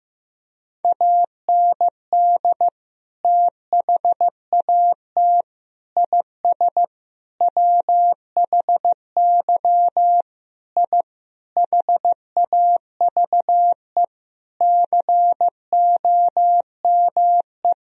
Morse code, heard as a keyed tone, about 15 wpm, 700 Hz. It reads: AND THAT IS WHY I HAVE COME
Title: Sentences from Top 1000 Words - Review Part 3 - 15wpm